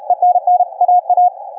Morse code, heard as a keyed tone, about 35 wpm, 720 Hz. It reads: ENN AA
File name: kf6mwa / cqww cw3